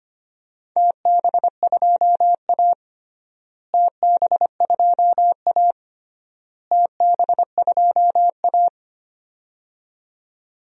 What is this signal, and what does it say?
Morse code, heard as a keyed tone, about 25 wpm, 700 Hz. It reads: TB2A TB2A TB2A
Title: DXCC-100 Call Signs - 4 Characters - Repeated 3x - 25wpm